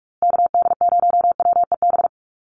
Morse code, heard as a keyed tone, about 44 wpm, 700 Hz. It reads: KD0WEB
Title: USA Call Signs Repeated 2x 45wpm